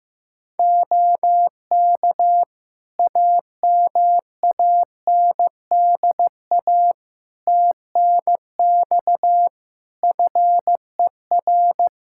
Morse code, heard as a keyed tone, about 15 wpm, 700 Hz. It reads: OK AMANDA TNX FER
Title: Practice QSO - Exchange 4 - Ending - 15wpm